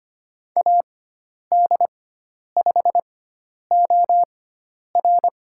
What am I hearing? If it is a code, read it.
Morse code, heard as a keyed tone, about 25 wpm, spaced as for 10 wpm, 700 Hz. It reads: AD5OR
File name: Lesson 16g - Letter c - Call Signs 25wpm